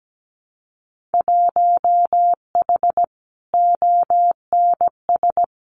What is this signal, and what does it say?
Morse code, heard as a keyed tone, about 17 wpm, 700 Hz. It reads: 1H ONS
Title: Winter Field Day Exchanges 17wpm